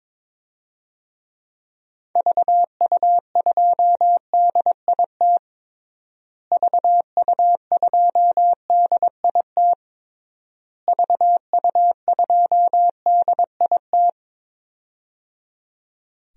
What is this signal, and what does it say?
Morse code, heard as a keyed tone, about 21 wpm, 700 Hz. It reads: VU2DIT VU2DIT VU2DIT
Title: DXCC-100 Call Signs - 6 Characters - Repeated 3x - 22wpm